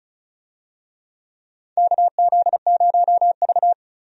Morse code, heard as a keyed tone, about 36 wpm, 700 Hz. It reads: KZ0V
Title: USA Call Signs - 4 Characters - 35wpm